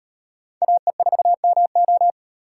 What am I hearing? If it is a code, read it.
Morse code, heard as a keyed tone, about 38 wpm, 700 Hz. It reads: AE4MO